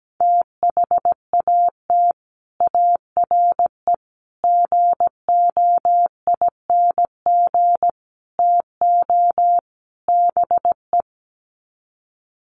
Morse code, heard as a keyed tone, about 17 wpm, 700 Hz. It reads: THAT ARE GOING TO BE
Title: Sets of 5 Words Encore2 17wpm